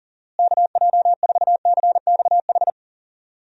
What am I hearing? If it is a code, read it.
Morse code, heard as a keyed tone, about 40 wpm, 700 Hz. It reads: KJ4CXH
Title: USA Call Signs - 6 Characters - Repeated 3x - 40wpm